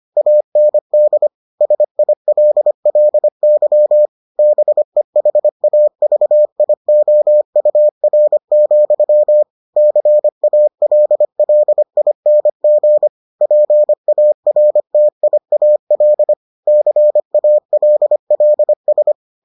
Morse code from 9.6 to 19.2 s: CALLING PARTIAL CALLS

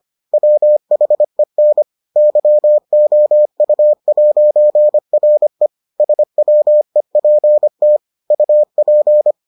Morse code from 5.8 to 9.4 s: SWEPT UP

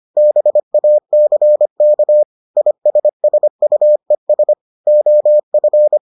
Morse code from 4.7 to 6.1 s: OF

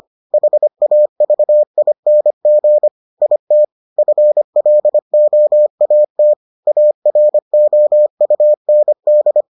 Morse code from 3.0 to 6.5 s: IT FLOAT

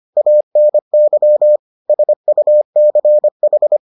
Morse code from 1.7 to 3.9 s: SUCH

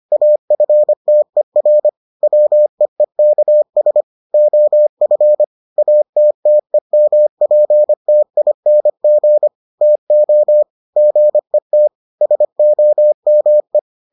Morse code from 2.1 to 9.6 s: WEEKS OF ATTEMPTING

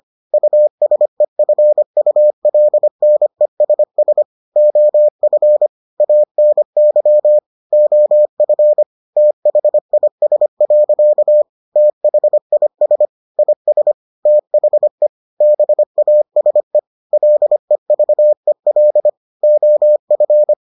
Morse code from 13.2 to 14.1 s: IS

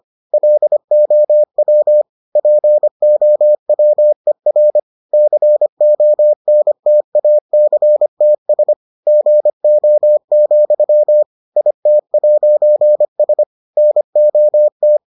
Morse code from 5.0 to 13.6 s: CONTACTS GO, IT'S

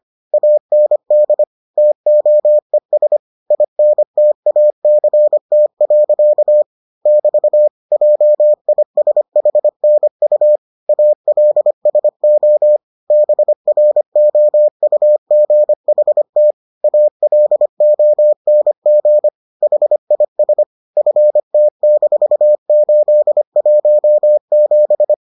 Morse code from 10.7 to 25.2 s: ALSO BROUGHT ALONG HIS FT-817

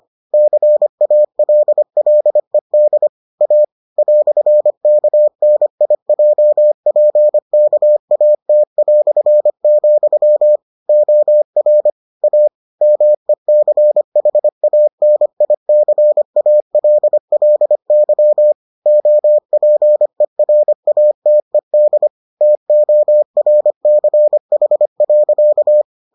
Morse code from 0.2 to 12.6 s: CALLED A "KNIJPKAT", OR A

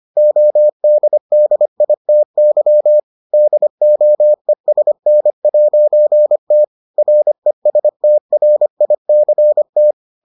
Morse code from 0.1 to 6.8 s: ODDITY DOESN'T

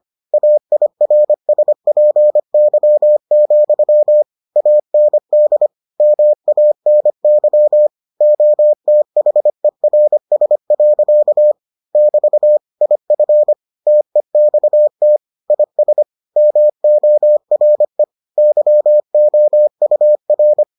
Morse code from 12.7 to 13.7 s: IF

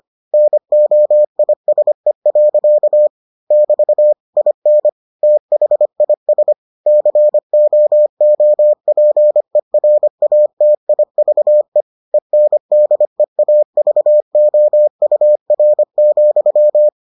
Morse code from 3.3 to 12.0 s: <BT> IN THIS COOPERATIVE

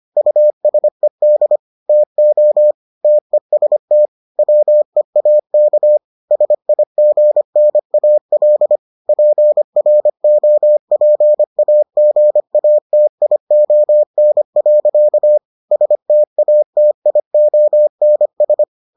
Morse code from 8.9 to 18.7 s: PROPAGATION. STATIONS

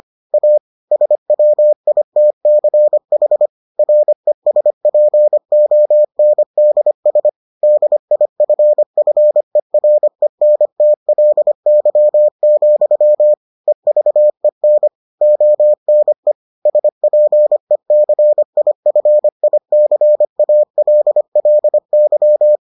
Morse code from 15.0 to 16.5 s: ONE

